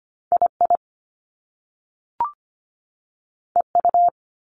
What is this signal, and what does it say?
Morse code, heard as a keyed tone, about 25 wpm, 700 Hz. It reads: II EU
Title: Top Two Letter Combinations in Words 25wpm